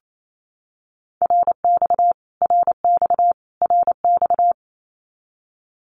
Morse code, heard as a keyed tone, about 28 wpm, 700 Hz. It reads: RX RX RX
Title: Abbreviations Repeated 3x - 28wpm